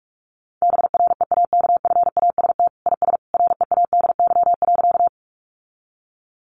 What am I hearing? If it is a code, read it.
Morse code, heard as a keyed tone, about 44 wpm, 700 Hz. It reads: BREAKFAST IS READY.
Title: Sentences from Top 1500 Words 45wpm